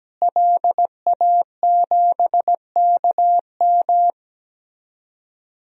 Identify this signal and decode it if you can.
Morse code, heard as a keyed tone, about 17 wpm, 700 Hz. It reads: LA7KM
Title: DXCC-100 Call Signs - 5 Characters - Repeated 2x - 17wpm